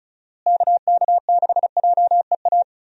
Morse code, heard as a keyed tone, about 36 wpm, 700 Hz. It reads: KK6JEA
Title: USA Call Signs - 6 Characters - Repeated 2x - 35wpm